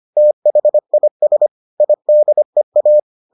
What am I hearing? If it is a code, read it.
Morse code, heard as a keyed tone, about 25 wpm, 600 Hz. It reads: THIS IDEA